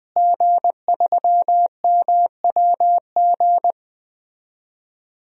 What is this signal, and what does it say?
Morse code, heard as a keyed tone, about 20 wpm, 700 Hz. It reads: G3MWG
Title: DXCC-100 Call Signs - 5 Characters - Repeated 3x - 20wpm